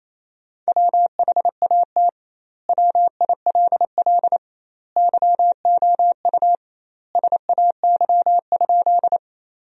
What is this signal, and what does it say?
Morse code, heard as a keyed tone, about 28 wpm, 700 Hz. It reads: WHAT WILL YOU SAY?
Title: 2x Word-Spacing - Sentences from Top 200 Words - 28wpm